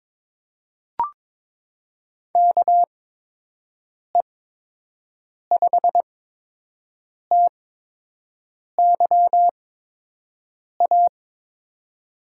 Morse code, heard as a keyed tone, about 21 wpm, spaced as for 6 wpm, 700 Hz. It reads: KE5TYA
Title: USA Call Signs - 6 Characters - 8x Character Spacing - 22wpm